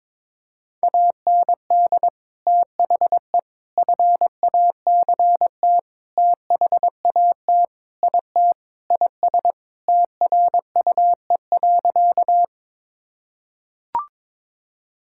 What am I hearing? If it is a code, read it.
Morse code, heard as a keyed tone, about 22 wpm, 700 Hz. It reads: AND THE FACT THAT IT IS TRUE.
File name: practice / 22wpm / Sentences with 4-Word Phrases - 22wpm